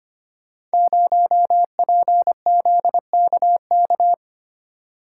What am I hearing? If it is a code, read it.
Morse code, heard as a keyed tone, about 25 wpm, 700 Hz. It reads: 0PZKK
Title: Code Groups - 5 Letter:Numbers - 25wpm